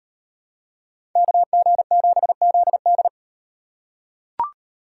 Morse code, heard as a keyed tone, about 38 wpm, 700 Hz. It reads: KG7ZD